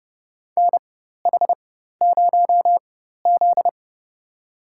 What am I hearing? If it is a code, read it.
Morse code, heard as a keyed tone, about 29 wpm, spaced as for 14 wpm, 700 Hz. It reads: NH0Z
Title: USA Call Signs - 4x Character Spacing - 30wpm